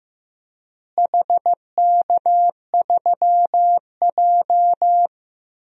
Morse code, heard as a keyed tone, about 15 wpm, 700 Hz. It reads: HK3J
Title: DXCC-100 Call Signs - 4 Characters - Repeated 3x - 15wpm